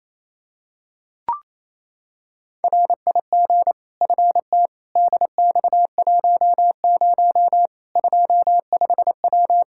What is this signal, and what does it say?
Morse code, heard as a keyed tone, about 28 wpm, 700 Hz. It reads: RIG FT DX10 25W
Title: Practice QSO - Exchange 2 - Info - 28wpm